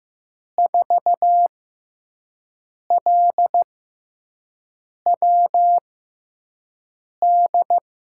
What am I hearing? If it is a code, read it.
Morse code, heard as a keyed tone, about 15 wpm, spaced as for 5 wpm, 700 Hz. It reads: 4LWD
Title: USA Call Signs - 6 Characters - 6x Character Spacing - 15wpm